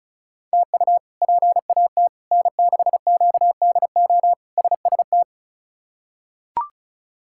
Morse code, heard as a keyed tone, about 36 wpm, 700 Hz. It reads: TU PAT N6QDO SST
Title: K1USN SST QSO Practice - 35wpm